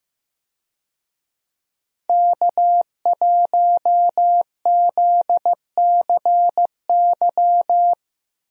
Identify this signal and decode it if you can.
Morse code, heard as a keyed tone, about 15 wpm, 700 Hz. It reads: K1ZCY
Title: USA Call Signs - 5 Characters - Repeated 3x - 15wpm